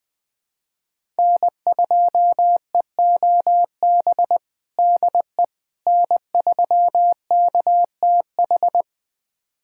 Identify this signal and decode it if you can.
Morse code, heard as a keyed tone, about 19 wpm, 700 Hz. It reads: N2EOB DE N3KTH
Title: USA Callsign DE Callsign - 5 Characters - 20wpm